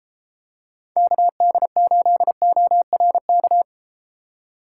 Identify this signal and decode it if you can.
Morse code, heard as a keyed tone, about 32 wpm, 700 Hz. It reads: KD8ORK